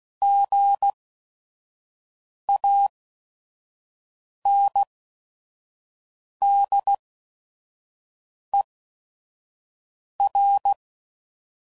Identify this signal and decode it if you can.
Morse code, heard as a keyed tone, about 16 wpm, spaced as for 5 wpm, 800 Hz. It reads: GANDER
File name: Prefixes / By First Letter / Gs